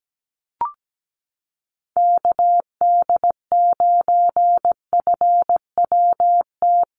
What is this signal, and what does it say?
Morse code, heard as a keyed tone, about 17 wpm, 700 Hz. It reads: KD9FWT
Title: USA Call Signs Repeated 2x 17wpm